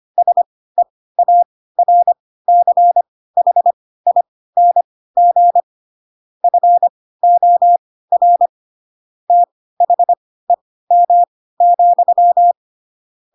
Morse code, heard as a keyed tone, about 25 wpm, spaced as for 16 wpm, 700 Hz. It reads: SEARCHING FOR THEM,
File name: Pinocchio / Pinocchio ch3 10 25 ik1hge